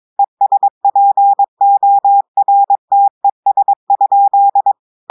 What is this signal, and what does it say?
Morse code, heard as a keyed tone, about 22 wpm, 800 Hz. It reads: ESPORTES?
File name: audios / portugais / portugais texte en clair 2 vitesse 22 mots minutes